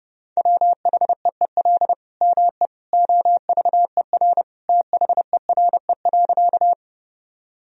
Morse code, heard as a keyed tone, about 30 wpm, 700 Hz. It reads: WHEEL ME OVER THERE.
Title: Sentences from Top 1000 Words - Review Part 3 - 30wpm